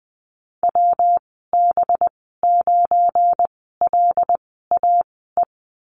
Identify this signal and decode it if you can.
Morse code, heard as a keyed tone, about 20 wpm, spaced as for 15 wpm, 700 Hz. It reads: WB9LAE
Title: 193 Lesson 31g - Letter v - Call Signs 20wpm